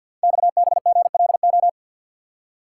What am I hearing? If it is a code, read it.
Morse code, heard as a keyed tone, about 48 wpm, 700 Hz. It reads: XBGLO
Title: Code Groups - 5 Letter:Numbers - 50wpm